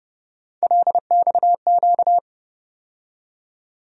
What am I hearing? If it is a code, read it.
Morse code, heard as a keyed tone, about 30 wpm, 700 Hz. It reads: LXQ
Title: USA Call Sign Postfixes - 3 Letters 30wpm